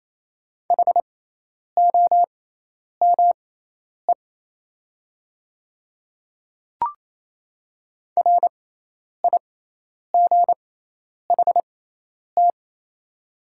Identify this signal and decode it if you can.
Morse code, heard as a keyed tone, about 28 wpm, spaced as for 10 wpm, 700 Hz. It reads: HOME RIGHT
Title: 6x Character Spacing - Top 200 Words - 28wpm